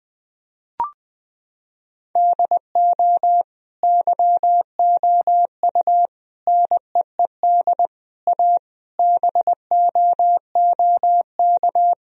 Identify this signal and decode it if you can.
Morse code, heard as a keyed tone, about 20 wpm, 700 Hz. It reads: DO YOU NEED A BOOK
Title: Sentences from Top 1000 Words - Review Part 4 - 20wpm